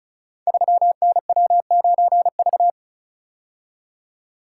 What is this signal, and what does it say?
Morse code, heard as a keyed tone, about 36 wpm, 700 Hz. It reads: 3NW9V
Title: Code Groups - 5 Letter:Numbers - 35wpm